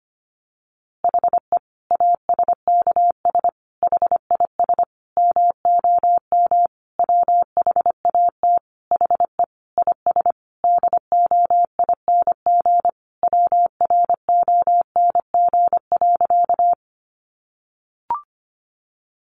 Morse code, heard as a keyed tone, about 25 wpm, 700 Hz. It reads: HE ASKS HIS MOM WHAT HE IS DOING WRONG.
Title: Easy Phrases 25wpm